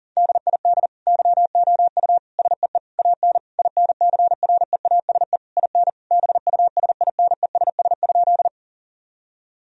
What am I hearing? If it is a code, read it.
Morse code, heard as a keyed tone, about 37 wpm, 700 Hz. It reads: DID YOU SEE AN INCREASE IN BUSINESS?
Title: Sentences from Top 1000 Words - Review Part 4 - 40wpm